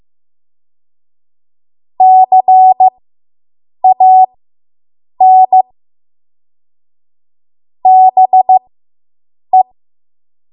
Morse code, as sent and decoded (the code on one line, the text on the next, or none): -.-. .- -. / -... .
CAN BE